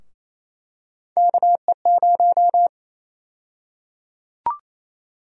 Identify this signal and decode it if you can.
Morse code, heard as a keyed tone, about 28 wpm, 700 Hz. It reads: KE0
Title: USA Call Sign Prefixes - Repeated 2x - 28wpm